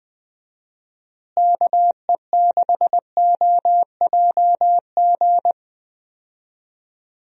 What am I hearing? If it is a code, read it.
Morse code, heard as a keyed tone, about 20 wpm, 700 Hz. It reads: KE6OJG
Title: USA Call Signs - 6 Characters - Repeated 3x - 20wpm